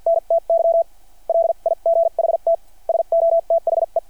Morse code, heard as a keyed tone, about 51 wpm, spaced as for 41 wpm, 660 Hz. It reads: NTY LIGHT SO THE
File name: archived / CRSnet